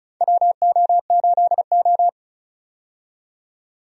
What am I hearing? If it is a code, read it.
Morse code, heard as a keyed tone, about 36 wpm, 700 Hz. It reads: WO8O
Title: USA Call Signs - 4 Characters - 35wpm